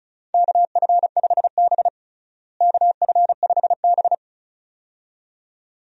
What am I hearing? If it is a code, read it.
Morse code, heard as a keyed tone, about 36 wpm, 700 Hz. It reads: KF5B KF5B
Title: USA Call Signs - 4 Characters - Repeated 2x - 35wpm